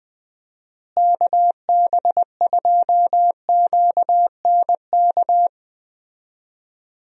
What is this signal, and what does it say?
Morse code, heard as a keyed tone, about 20 wpm, 700 Hz. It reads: KB2QNK
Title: USA Call Signs - 6 Characters - Repeated 3x - 20wpm